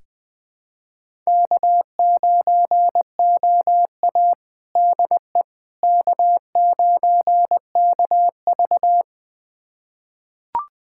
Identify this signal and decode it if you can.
Morse code, heard as a keyed tone, about 20 wpm, 700 Hz. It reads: K9OA DE K9KV
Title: USA Callsign DE Callsign - 4 Characters - 20wpm